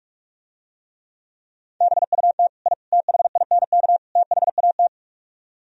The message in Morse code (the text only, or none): BUT I THINK THAT